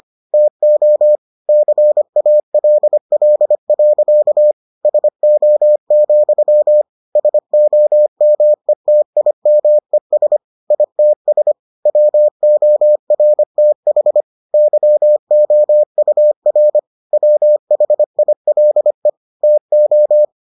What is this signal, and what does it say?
Morse code, heard as a keyed tone, about 25 wpm, 600 Hz. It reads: TO CALL. SO, SOMETIMES ITS WORTH YOUR WHILE TO